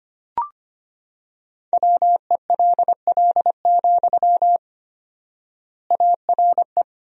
Morse code, heard as a keyed tone, about 25 wpm, 700 Hz. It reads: WELL, ARE
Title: 4x Word-Spacing - Sentences from Top 400 Words - 25wpm